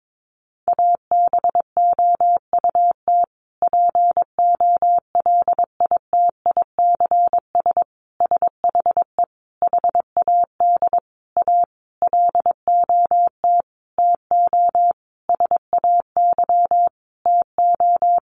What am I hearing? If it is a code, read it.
Morse code, heard as a keyed tone, about 22 wpm, 700 Hz. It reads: ABOUT POLITICS SHE HAD A LOT TO SAY TO
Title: Sentences from Top 1000 Words 22wpm